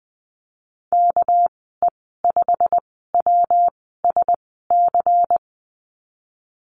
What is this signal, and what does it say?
Morse code, heard as a keyed tone, about 19 wpm, spaced as for 14 wpm, 700 Hz. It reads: KE5WSC
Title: 231 Lesson 37g - Letter x - Call Signs 20wpm